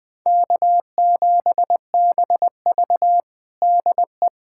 Morse code, heard as a keyed tone, about 20 wpm, 700 Hz. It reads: K7BV DE
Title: Practice QSO - Exchange 4 - Ending - 20wpm